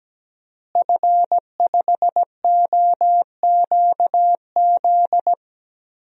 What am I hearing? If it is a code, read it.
Morse code, heard as a keyed tone, about 17 wpm, 700 Hz. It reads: F5OQZ
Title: DXCC-100 Call Signs - 5 Characters - Repeated 2x - 17wpm